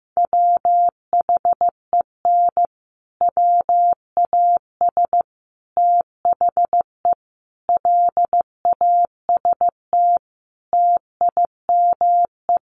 Morse code, heard as a keyed tone, about 15 wpm, 700 Hz. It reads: WHEN WAS THE LAST TIME
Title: Sets of 5 Words Encore 15wpm